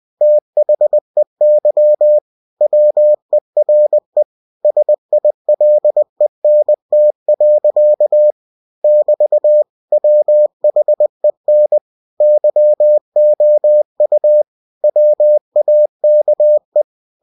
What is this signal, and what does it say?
Morse code, heard as a keyed tone, about 20 wpm, 600 Hz. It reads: THEY WERE SILENT. <BT> WHEN YOU WAKE
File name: cw-books / winnie-the-pooh / winnie-20wpm-page-0110